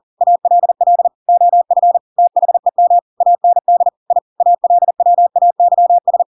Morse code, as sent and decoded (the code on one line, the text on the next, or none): .- .-.. .-.. / --- ..-. / - .... -- / .- -. -.. / .. / .- .-.. .-- .- -.-- ...
ALL OF THEM AND I ALWAYS